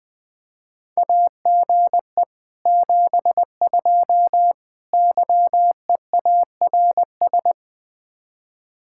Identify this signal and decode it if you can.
Morse code, heard as a keyed tone, about 20 wpm, 700 Hz. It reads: AGE 72 YEARS
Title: Speed-Racing Common QSO Phrases - 17wpm